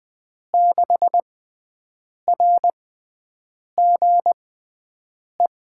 Morse code, heard as a keyed tone, about 20 wpm, spaced as for 7 wpm, 700 Hz. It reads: 6RGE
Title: USA Call Signs - 5 Characters - 6x Character Spacing - 20wpm